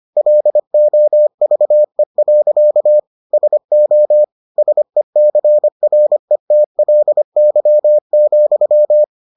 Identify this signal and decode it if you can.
Morse code, heard as a keyed tone, about 25 wpm, 600 Hz. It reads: LOVE. SO SECRETLY,